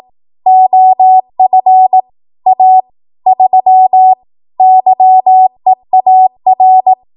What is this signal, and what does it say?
Morse code, heard as a keyed tone, about 18 wpm, 750 Hz. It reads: OF A 3 YEAR
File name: Archive / 18 WPM / 18WPM